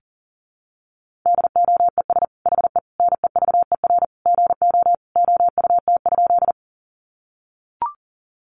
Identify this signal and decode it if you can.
Morse code, heard as a keyed tone, about 37 wpm, 700 Hz. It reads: DOES HE NEVER GO OUT?
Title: Sentences from Top 300 Words 40wpm